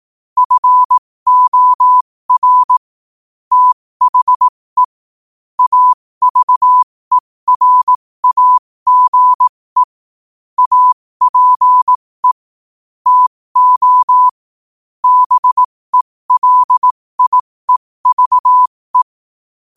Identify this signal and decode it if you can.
Morse code, heard as a keyed tone, about 18 wpm, spaced as for 15 wpm, 1000 Hz. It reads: FOR THE AVERAGE APE TO BELIEVE
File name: qotdfiles / 15wpm